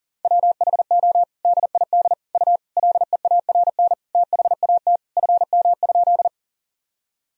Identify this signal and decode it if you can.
Morse code, heard as a keyed tone, about 40 wpm, 700 Hz. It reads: WHO DID U LEARN THAT FM?
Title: Sentences from Top 300 Words and Abbreviations 40wpm